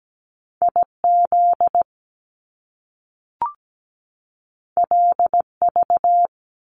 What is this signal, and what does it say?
Morse code, heard as a keyed tone, about 17 wpm, 700 Hz. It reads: IZ LV
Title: Top Two Letter Combinations in Words 17wpm